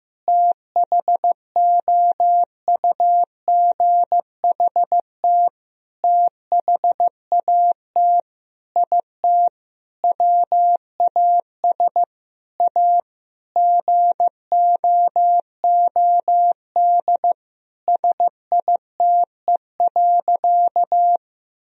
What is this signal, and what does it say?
Morse code, heard as a keyed tone, about 15 wpm, 700 Hz. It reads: THOUGHT THAT IT WAS A GOOD SITE.
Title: Sentences from Top 1000 Words - Review Part 4 - 15wpm